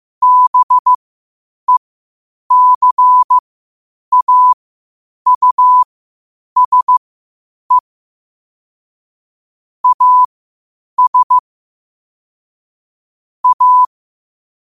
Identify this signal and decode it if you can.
Morse code, heard as a keyed tone, about 15 wpm, spaced as for 8 wpm, 1000 Hz. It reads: BECAUSE AS A
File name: qotdfiles / 8wpm